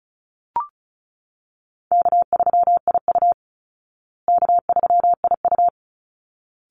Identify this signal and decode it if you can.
Morse code, heard as a keyed tone, about 34 wpm, 700 Hz. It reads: K3IU K3IU
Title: Easy Call Signs Repeated 2x - 35wpm